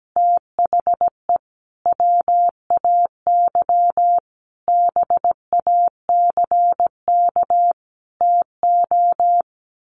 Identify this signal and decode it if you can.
Morse code, heard as a keyed tone, about 17 wpm, 700 Hz. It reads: THE WAY BACK TO
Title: A Sound Of Thunder 17wpm